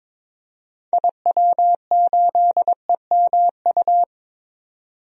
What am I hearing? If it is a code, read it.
Morse code, heard as a keyed tone, about 22 wpm, 700 Hz. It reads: IW8EMU
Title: DXCC-100 Call Signs - 6 Characters - 22wpm